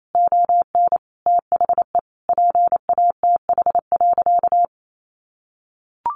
Morse code, heard as a keyed tone, about 28 wpm, 700 Hz. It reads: ON THE PATH.
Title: A Sound Of Thunder 28wpm